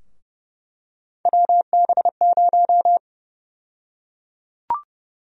Morse code, heard as a keyed tone, about 30 wpm, 700 Hz. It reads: WB0
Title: USA Call Sign Prefixes - Repeated 2x - 30wpm